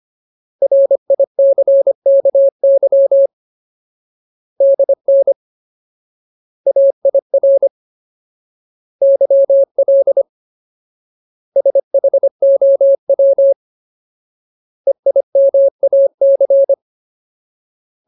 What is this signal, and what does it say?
Morse code, heard as a keyed tone, about 25 wpm, 550 Hz. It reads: RICKY DN AIR YL SHOW EIMAC
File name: Ultimate Word List / 25WPM Ultimate